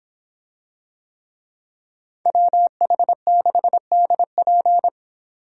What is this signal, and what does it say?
Morse code, heard as a keyed tone, about 26 wpm, 700 Hz. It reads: WH6DP